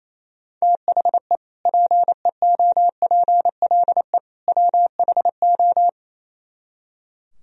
Morse code, heard as a keyed tone, about 34 wpm, 700 Hz. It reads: THE PEOPLE WHO
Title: Speed-Racing - Sets of 3 Words 28wpm